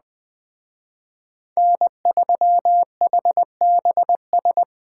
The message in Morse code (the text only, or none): N3HBS